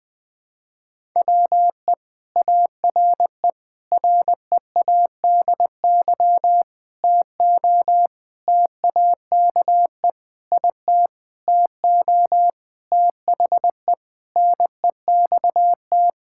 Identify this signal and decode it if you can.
Morse code, heard as a keyed tone, about 20 wpm, 700 Hz. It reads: WE ARE READY TO TAKE IT TO THE NEXT
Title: Sentences with 7-Word Phrases - 20wpm